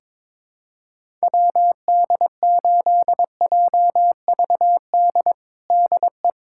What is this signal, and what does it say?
Morse code, heard as a keyed tone, about 22 wpm, 700 Hz. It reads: WD8JVD DE
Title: USA Callsign DE Callsign 22wpm